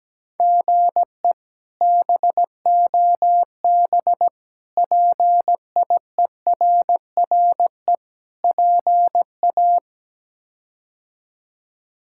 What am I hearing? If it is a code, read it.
Morse code, heard as a keyed tone, about 17 wpm, 700 Hz. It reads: GE BOB PIERRE PA
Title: K1USN SST QSO Practice - 17wpm